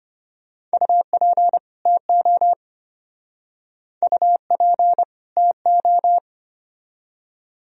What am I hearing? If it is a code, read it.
Morse code, heard as a keyed tone, about 33 wpm, 700 Hz. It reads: UP TO UP TO